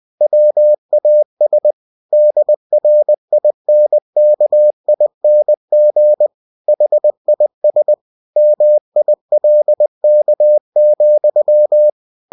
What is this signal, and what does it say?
Morse code, heard as a keyed tone, about 20 wpm, 600 Hz. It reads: WAS DRINKING HIS MILK,
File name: cw-books / winnie-the-pooh / winnie-20wpm-page-0107